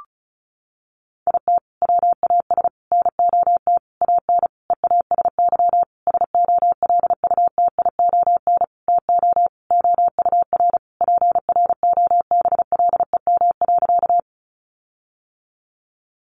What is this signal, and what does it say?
Morse code, heard as a keyed tone, about 34 wpm, 700 Hz. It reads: IT WAS NOT AN EASY SOLUTION TO OUR PROBLEM.
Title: Sentences from Top 1300 Words 35wpm